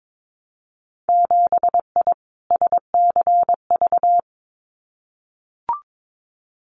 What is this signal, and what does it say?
Morse code, heard as a keyed tone, about 22 wpm, 700 Hz. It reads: 7I SCV
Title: Winter Field Day Exchanges Encore 22wpm